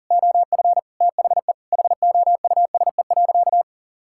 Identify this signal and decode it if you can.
Morse code, heard as a keyed tone, about 40 wpm, 700 Hz. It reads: OF THE HOUSE.
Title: Sentences from Top 1000 Words - Review Part 6 - 40wpm